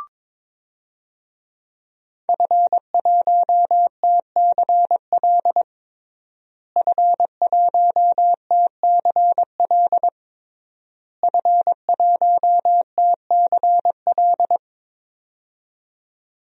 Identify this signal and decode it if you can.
Morse code, heard as a keyed tone, about 22 wpm, 700 Hz. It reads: F1TCL F1TCL F1TCL
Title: DXCC-100 Call Signs - 5 Characters - Repeated 3x - 22wpm